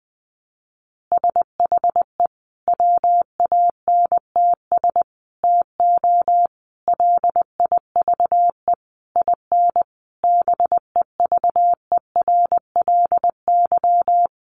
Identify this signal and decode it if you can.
Morse code, heard as a keyed tone, about 20 wpm, 700 Hz. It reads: SHE WANTS TO LIVE IN BEVERLY
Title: Easy Phrases 20wpm